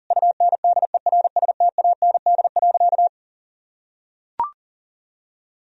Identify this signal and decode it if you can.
Morse code, heard as a keyed tone, about 40 wpm, 700 Hz. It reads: UNDERSTAND.
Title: Sentences from Top 1000 Words - Review Part 1 - 40wpm